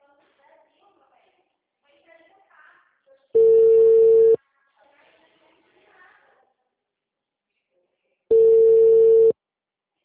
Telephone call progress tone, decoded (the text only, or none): ringback tone